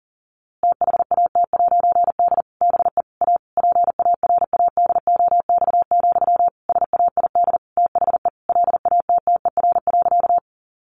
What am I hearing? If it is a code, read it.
Morse code, heard as a keyed tone, about 40 wpm, 700 Hz. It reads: THAT'D BE A PARADOX, SAID THE LATTER.
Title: A Sound Of Thunder 40wpm